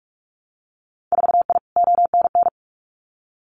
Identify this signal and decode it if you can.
Morse code, heard as a keyed tone, about 46 wpm, 690 Hz. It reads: 4I ONN
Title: Winter Field Day Exchanges Encore 45wpm